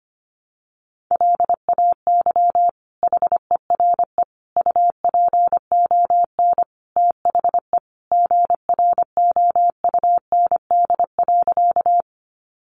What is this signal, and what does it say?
Morse code, heard as a keyed tone, about 25 wpm, 700 Hz. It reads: LAY HERE UPON THE GROUND.
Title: Sentences from Top 1100 Words 25wpm